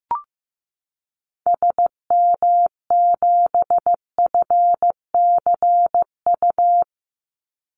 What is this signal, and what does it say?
Morse code, heard as a keyed tone, about 15 wpm, 700 Hz. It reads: SM7FCU
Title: Call Signs Repeated 2x - 15wpm